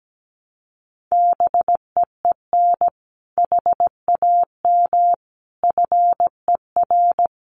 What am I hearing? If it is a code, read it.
Morse code, heard as a keyed tone, about 17 wpm, 700 Hz. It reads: BEEN HAM FER
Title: Common QSO Phrases Encore 17wpm